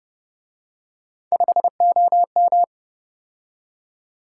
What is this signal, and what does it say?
Morse code, heard as a keyed tone, about 31 wpm, 700 Hz. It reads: 5OM